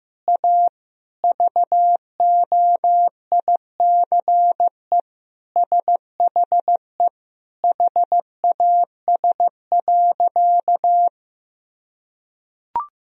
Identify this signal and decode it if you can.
Morse code, heard as a keyed tone, about 15 wpm, 700 Hz. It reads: A VOICE SHE HAS.
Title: Sentences from Top 500 Words and Abbreviations 15wpm